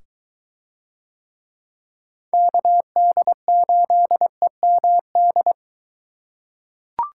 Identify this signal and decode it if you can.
Morse code, heard as a keyed tone, about 23 wpm, 700 Hz. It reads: KD8EMD